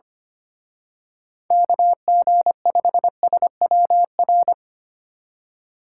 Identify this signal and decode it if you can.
Morse code, heard as a keyed tone, about 28 wpm, 700 Hz. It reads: KG5SWR
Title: Speed Racing - USA 2x3 Call Signs 25wpm